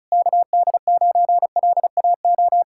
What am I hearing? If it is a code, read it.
Morse code, heard as a keyed tone, about 42 wpm, 700 Hz. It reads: KD9LAO
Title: Speed-Racing Common QSO Phrases - 35wpm